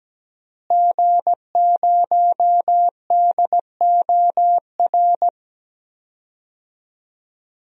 Morse code, heard as a keyed tone, about 17 wpm, 700 Hz. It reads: G0DOR
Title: DXCC-100 Call Signs - 5 Characters - Repeated 2x - 17wpm